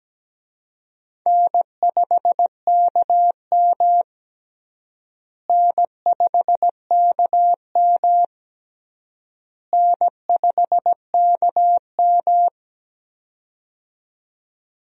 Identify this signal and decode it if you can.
Morse code, heard as a keyed tone, about 17 wpm, 700 Hz. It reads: N5KM N5KM N5KM
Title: USA Call Signs - 4 Characters - Repeated 3x - 17wpm